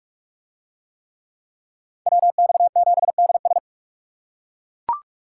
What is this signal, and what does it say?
Morse code, heard as a keyed tone, about 40 wpm, 700 Hz. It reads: WX7DS